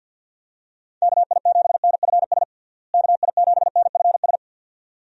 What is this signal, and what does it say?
Morse code, heard as a keyed tone, about 50 wpm, 700 Hz. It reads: KI7NFS KI7NFS
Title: USA Call Signs - 6 Characters - Repeated 2x - 50wpm